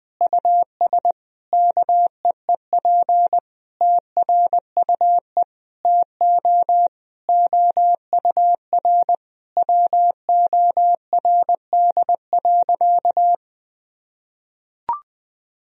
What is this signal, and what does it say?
Morse code, heard as a keyed tone, about 20 wpm, 700 Hz. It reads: US KEEP TRUE TO OUR WORD.
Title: Sentences from Top 1000 Words - Review Part 3 - 20wpm